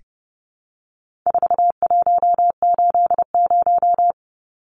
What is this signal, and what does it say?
Morse code, heard as a keyed tone, about 28 wpm, 700 Hz. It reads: 4180
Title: Numbers - 4-Digit 30wpm